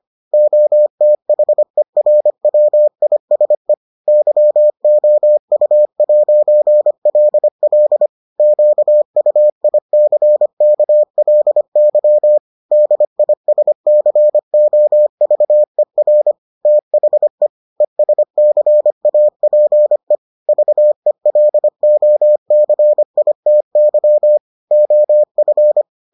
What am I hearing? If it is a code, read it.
Morse code, heard as a keyed tone, about 25 wpm, 600 Hz. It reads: OTHERWISE YOU'LL QUICKLY DISCOVER THE ESCAPE VELOCITY OF